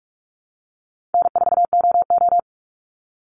Morse code, heard as a keyed tone, about 44 wpm, 700 Hz. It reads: N4OO
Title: Call Signs - 45wpm